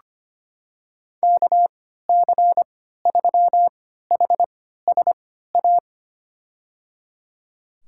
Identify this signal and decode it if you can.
Morse code, heard as a keyed tone, about 25 wpm, spaced as for 14 wpm, 700 Hz. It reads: KC3HSA
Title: USA Call Signs - 6 Characters - 3x Character Spacing - 25wpm